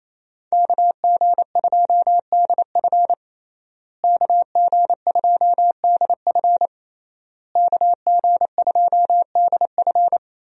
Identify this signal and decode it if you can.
Morse code, heard as a keyed tone, about 28 wpm, 700 Hz. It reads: KG2DF KG2DF KG2DF
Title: USA Call Signs - 5 Characters - Repeated 3x - 28wpm